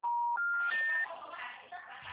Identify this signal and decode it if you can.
Special information tone, with speech.
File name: out- -2020- - - .213